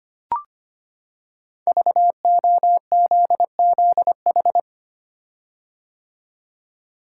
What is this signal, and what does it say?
Morse code, heard as a keyed tone, about 25 wpm, 700 Hz. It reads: VOZZH
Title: Code Groups - 5 Letter:Numbers - 25wpm